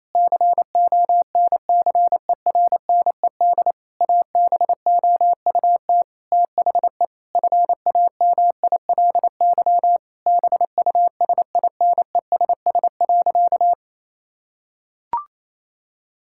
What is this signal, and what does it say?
Morse code, heard as a keyed tone, about 28 wpm, 700 Hz. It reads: CONCERNED ABOUT THE FAMILY BUSINESS.
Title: Sentences from Top 1000 Words - Review Part 6 - 28wpm